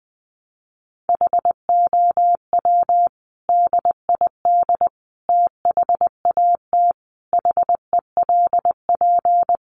Morse code, heard as a keyed tone, about 20 wpm, 700 Hz. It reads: HOW DID THAT HELP
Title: Sentences from Top 200 Words 20wpm